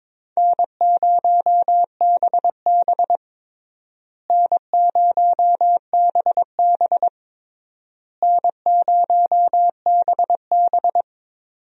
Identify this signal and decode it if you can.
Morse code, heard as a keyed tone, about 22 wpm, 700 Hz. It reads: N0BB N0BB N0BB
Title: USA Call Signs - 4 Characters - Repeated 3x - 22wpm